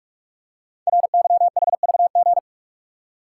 Morse code, heard as a keyed tone, about 44 wpm, 690 Hz. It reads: RYHVG
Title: Code Groups - 5 Letters - 45wpm